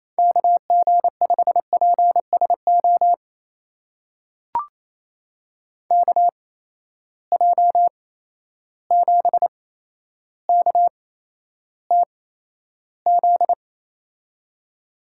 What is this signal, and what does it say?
Morse code, heard as a keyed tone, about 28 wpm, spaced as for 12 wpm, 700 Hz. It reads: KG5PSO KJ7KTZ
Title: USA Call Signs - 8x Character Spacing - 28wpm